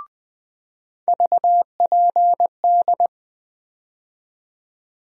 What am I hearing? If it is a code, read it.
Morse code, heard as a keyed tone, about 20 wpm, 700 Hz. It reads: VPD
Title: USA Call Sign Postfixes - 3 Letters 20wpm